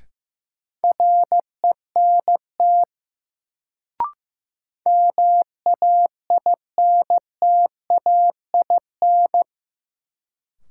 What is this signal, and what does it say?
Morse code, heard as a keyed tone, about 15 wpm, 700 Hz. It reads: RENT MAINTAIN
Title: Sentences from Top 1000 Words - Review Part 5 - New Word Drill - 15wpm